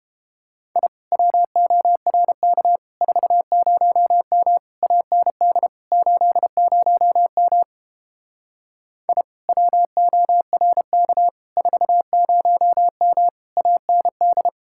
Morse code, heard as a keyed tone, about 31 wpm, 700 Hz. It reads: I WORK 40M AND 80M I WORK 40M AND